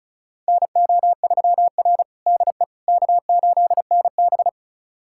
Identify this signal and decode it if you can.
Morse code, heard as a keyed tone, about 34 wpm, 700 Hz. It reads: NO3R DE K8NB
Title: USA Callsign DE Callsign - 4 Characters - 35wpm